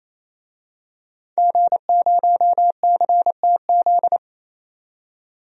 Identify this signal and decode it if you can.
Morse code, heard as a keyed tone, about 28 wpm, 700 Hz. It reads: G0CTZ
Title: DXCC-100 Call Signs - 5 Characters - 28wpm